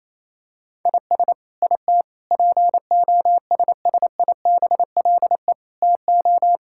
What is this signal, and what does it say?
Morse code, heard as a keyed tone, about 27 wpm, 700 Hz. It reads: IS IT POSSIBLE TO
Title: Sentences from Top 1000 Words - Review Part 5 - 28wpm